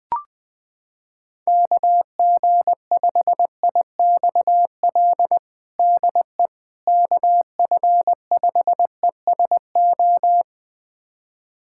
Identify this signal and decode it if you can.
Morse code, heard as a keyed tone, about 20 wpm, 700 Hz. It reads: KG5IXL DE KF5ESO
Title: USA Callsign DE Callsign - 6 Characters - 20wpm